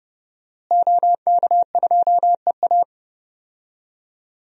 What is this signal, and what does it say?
Morse code, heard as a keyed tone, about 30 wpm, 700 Hz. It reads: OK2EA
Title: DXCC-100 Call Signs - 5 Characters - Repeated 2x - 30wpm